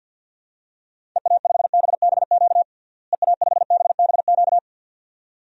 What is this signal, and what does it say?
Morse code, heard as a keyed tone, about 53 wpm, 700 Hz. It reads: EA5BBQ EA5BBQ
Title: DXCC-100 Call Signs - 6 Characters - Repeated 2x - 50wpm